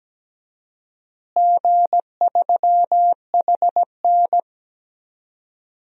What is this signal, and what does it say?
Morse code, heard as a keyed tone, about 17 wpm, 700 Hz. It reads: G3HN
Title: DXCC-100 Call Signs - 4 Characters - Repeated 3x - 17wpm